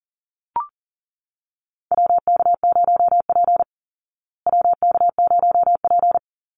Morse code, heard as a keyed tone, about 40 wpm, 700 Hz. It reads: WK0P WK0P
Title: Call Signs Repeated 2x - 40wpm